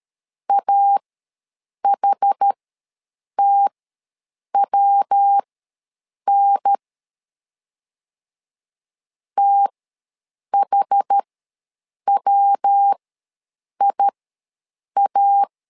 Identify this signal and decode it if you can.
Morse code, heard as a keyed tone, about 13 wpm, spaced as for 7 wpm, 800 Hz. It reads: AHTWN THWIA
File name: LEZIONI / 3 MWRO / 3 MW ws 12wpm 1816